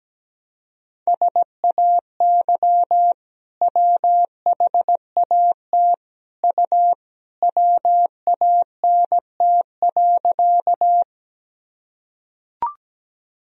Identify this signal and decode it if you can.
Morse code, heard as a keyed tone, about 17 wpm, 700 Hz. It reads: SAY WHAT U WANT.
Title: Sentences from Top 200 Words and Abbreviations 17wpm